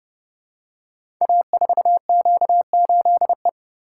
Morse code, heard as a keyed tone, about 30 wpm, 700 Hz. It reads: A4Q8E
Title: Code Groups - 5 Letter:Numbers - 30wpm